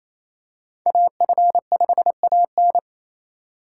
Morse code, heard as a keyed tone, about 28 wpm, 700 Hz. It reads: AF5AN
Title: USA Call Signs - 4x Character Spacing - 28wpm